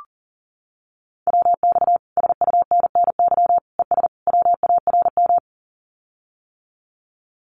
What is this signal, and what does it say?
Morse code, heard as a keyed tone, about 40 wpm, 700 Hz. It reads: WX SUNNY ES WARM
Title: Common QSO Phrases Encore 40wpm